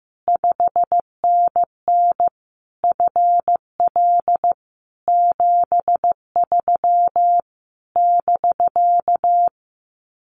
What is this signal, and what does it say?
Morse code, heard as a keyed tone, about 15 wpm, 700 Hz. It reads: 5NN FL 73 <BK>
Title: Common QSO Phrases Encore 15wpm